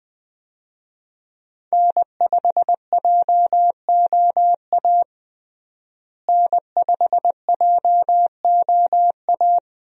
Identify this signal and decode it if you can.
Morse code, heard as a keyed tone, about 20 wpm, 700 Hz. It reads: N5JOA N5JOA
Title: USA Call Signs - 5 Characters - Repeated 3x - 20wpm